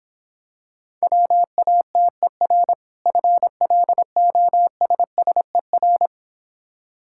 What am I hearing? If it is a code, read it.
Morse code, heard as a keyed tone, about 26 wpm, 700 Hz. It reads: WATER FLOSSER